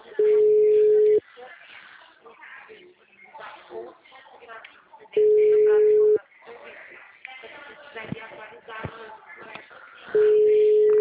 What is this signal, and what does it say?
Ringback tone (Europe, 425 Hz), with speech.